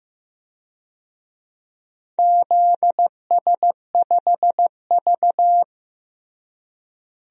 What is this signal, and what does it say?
Morse code, heard as a keyed tone, about 15 wpm, 700 Hz. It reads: ZS5V
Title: DXCC-100 Call Signs - 4 Characters - Repeated 2x - 15wpm